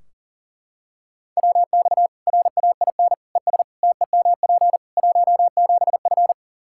Morse code, heard as a keyed tone, about 40 wpm, 700 Hz. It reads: WX RAIN ES TEMP 17F
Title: Practice QSO - Exchange 2 - Info - 40wpm